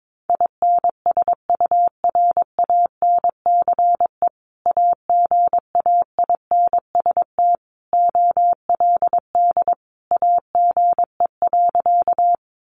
Morse code, heard as a keyed tone, about 22 wpm, 700 Hz. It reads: INSURANCE AGAINST OLD AGE.
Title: Sentences from Top 1100 Words 22wpm